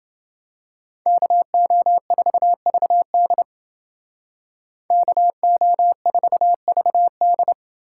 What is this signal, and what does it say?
Morse code, heard as a keyed tone, about 28 wpm, 700 Hz. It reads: KO4VD KO4VD